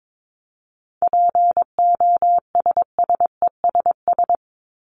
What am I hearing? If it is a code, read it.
Morse code, heard as a keyed tone, about 22 wpm, 700 Hz. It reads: POSSESS
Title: Top 1751-2000 Words 22wpm